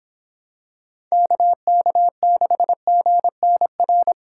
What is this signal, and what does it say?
Morse code, heard as a keyed tone, about 24 wpm, 700 Hz. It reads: KK6GNR